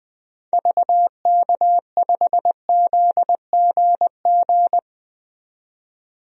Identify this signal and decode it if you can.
Morse code, heard as a keyed tone, about 20 wpm, 700 Hz. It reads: VK5ZGG
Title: DXCC-100 Call Signs - 6 Characters - 20wpm